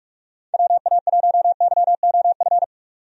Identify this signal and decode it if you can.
Morse code, heard as a keyed tone, about 43 wpm, 700 Hz. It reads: WA1YOF
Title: USA Call Signs - 6 Characters - 45wpm